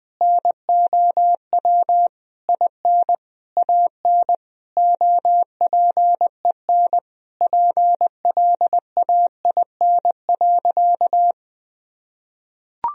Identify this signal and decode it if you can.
Morse code, heard as a keyed tone, about 20 wpm, 700 Hz. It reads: NOW IN AN OPEN PLAIN.
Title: Sentences from Top 1000 Words - Review Part 2 - 20wpm